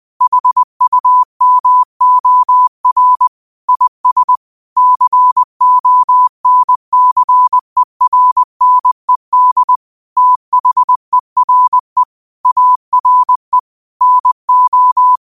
Morse code, heard as a keyed tone, about 20 wpm, 1000 Hz. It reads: HUMOR IS CONCERNED THERE ARE NO